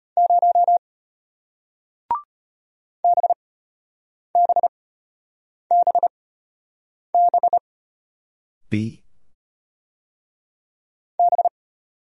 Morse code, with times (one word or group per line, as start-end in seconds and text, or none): 0.2-0.8 0
3.0-7.6 BBBB
11.2-11.5 B